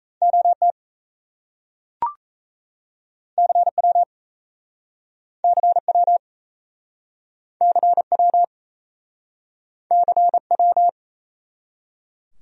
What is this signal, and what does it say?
Morse code, heard as a keyed tone, about 40 wpm, 700 Hz. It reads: OT CW CW CW CW